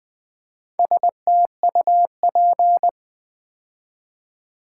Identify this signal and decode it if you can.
Morse code, heard as a keyed tone, about 20 wpm, 700 Hz. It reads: STUP
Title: Build A Word - 2 Letters At A Time - 6-Letter Words - 20wpm